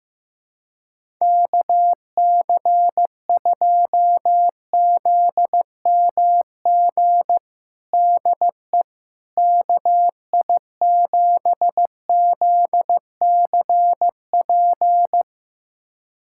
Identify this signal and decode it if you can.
Morse code, heard as a keyed tone, about 15 wpm, 700 Hz. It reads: KC2ZMG DE KI7ZCP
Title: USA Callsign DE Callsign - 6 Characters - 15wpm